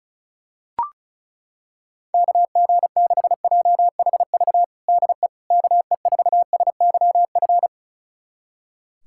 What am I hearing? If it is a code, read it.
Morse code, heard as a keyed tone, about 36 wpm, 700 Hz. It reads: KG6JHV DE KE4SYF
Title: USA Callsign DE Callsign - 6 Characters - 35wpm